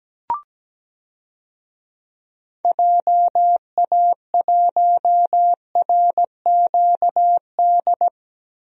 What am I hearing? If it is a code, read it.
Morse code, heard as a keyed tone, about 17 wpm, 700 Hz. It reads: JA1RQD
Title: DXCC-100 Call Signs - 6 Characters - Repeated 3x - 17wpm